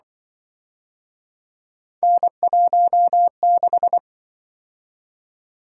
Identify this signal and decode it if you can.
Morse code, heard as a keyed tone, about 25 wpm, 700 Hz. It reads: N16
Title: Speed-Racing - Three Letter-Number Combinations - 20wpm